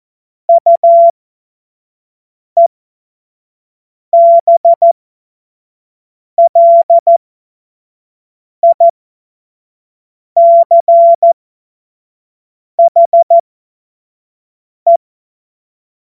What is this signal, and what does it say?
Morse code, heard as a keyed tone, about 14 wpm, spaced as for 5 wpm, 680 Hz. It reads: UEBLICHE
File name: Morsekurs / Witze / Witz 15. August 2023 25 F70